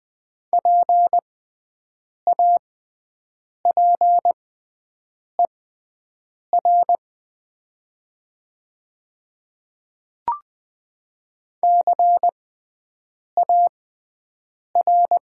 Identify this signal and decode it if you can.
Morse code, heard as a keyed tone, about 20 wpm, spaced as for 7 wpm, 700 Hz. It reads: PAPER CAR
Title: 6x Character Spacing - Top 300 Words - 20wpm